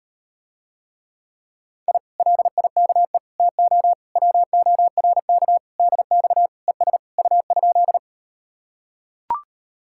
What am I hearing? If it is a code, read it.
Morse code, heard as a keyed tone, about 39 wpm, 700 Hz. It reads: I LIKE TO WORK DX ES U?